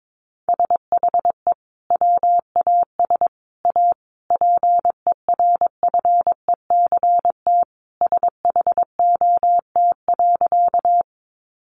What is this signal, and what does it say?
Morse code, heard as a keyed tone, about 22 wpm, 700 Hz. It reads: SHE WAS A PERFECT SHOT.
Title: Sentences from Top 1100 Words 22wpm